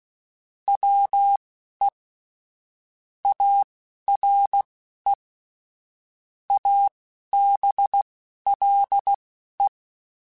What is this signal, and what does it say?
Morse code, heard as a keyed tone, about 16 wpm, spaced as for 11 wpm, 800 Hz. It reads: WE ARE ABLE
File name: QSOs / QSOs-mp3s / WA7QOX 1@10